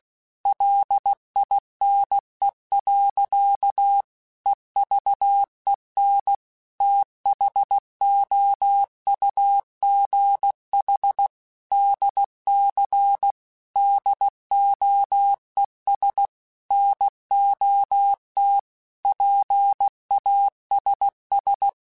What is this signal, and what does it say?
Morse code, heard as a keyed tone, about 16 wpm, 800 Hz. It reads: LINE. EVEN THOUGH DC DOES NOT PASS
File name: Technical Subjects / Technical Subjects-mp3s / Sw Pwr Sup @15